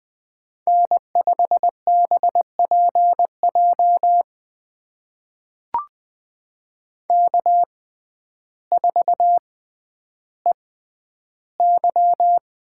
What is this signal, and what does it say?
Morse code, heard as a keyed tone, about 20 wpm, spaced as for 11 wpm, 700 Hz. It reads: N5BPJ K4EY
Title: USA Call Signs - 5 Characters - 6x Character Spacing - 20wpm